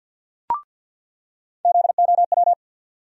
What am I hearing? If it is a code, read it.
Morse code, heard as a keyed tone, about 50 wpm, 700 Hz. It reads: ZOW
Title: USA Call Sign Postfixes - 3 Letters 50wpm